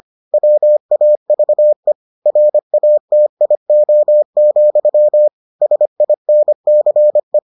Morse code, heard as a keyed tone, about 25 wpm, 600 Hz. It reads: WAVE RATIO, SINCE